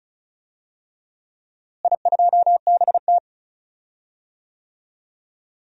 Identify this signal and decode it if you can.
Morse code, heard as a keyed tone, about 36 wpm, 700 Hz. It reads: I2BT
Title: DXCC-100 Call Signs - 4 Characters - Repeated 2x - 35wpm